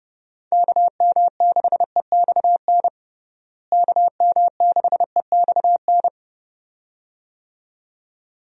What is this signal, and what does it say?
Morse code, heard as a keyed tone, about 30 wpm, 700 Hz. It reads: KM6EXN KM6EXN
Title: USA Call Signs - 6 Characters - Repeated 2x - 30wpm